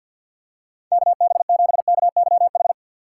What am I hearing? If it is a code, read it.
Morse code, heard as a keyed tone, about 48 wpm, 700 Hz. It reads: KB7KYH
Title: USA Call Signs - 3x Character Spacing - 50wpm